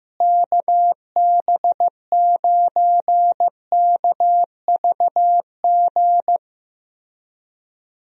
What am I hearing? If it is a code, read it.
Morse code, heard as a keyed tone, about 15 wpm, 700 Hz. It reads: KB9KVG
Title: USA Call Signs - 6 Characters - Repeated 3x - 15wpm